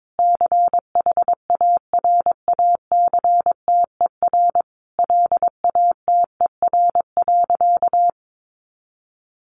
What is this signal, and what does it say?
Morse code, heard as a keyed tone, about 22 wpm, 700 Hz. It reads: CHARACTER LATER.
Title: Sentences from Top 1200 Words 22wpm